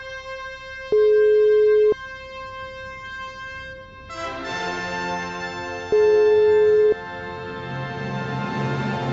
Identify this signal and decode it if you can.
Ringback tone (Europe, 425 Hz).